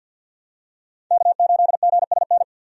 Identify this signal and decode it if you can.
Morse code, heard as a keyed tone, about 52 wpm, 700 Hz. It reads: K8GSN
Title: USA Call Signs - 5 Characters - 50wpm